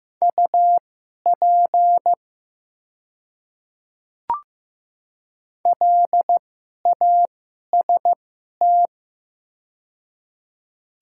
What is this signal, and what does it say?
Morse code, heard as a keyed tone, about 15 wpm, spaced as for 11 wpm, 700 Hz. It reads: UP LAST
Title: 2x Character Spacing - Top 500 Words - 15wpm